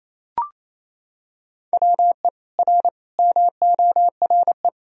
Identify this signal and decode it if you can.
Morse code, heard as a keyed tone, about 28 wpm, 700 Hz. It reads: WE R MORE
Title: Sentences from Top 500 Words and Abbreviations 28wpm